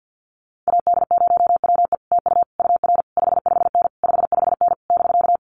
Morse code, heard as a keyed tone, about 50 wpm, 700 Hz. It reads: AD0WE TU UR 55N 55N <BK>
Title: Common QSO Phrases Encore 50wpm